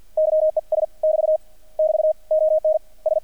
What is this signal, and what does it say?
Morse code, heard as a keyed tone, about 50 wpm, 620 Hz. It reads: YES <BT> <BT> ON S
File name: archived / CRSnet